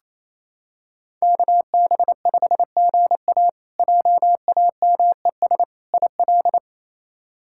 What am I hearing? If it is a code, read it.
Morse code, heard as a keyed tone, about 28 wpm, 700 Hz. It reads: KB5GA JAMES IL
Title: K1USN SST QSO Practice - 28wpm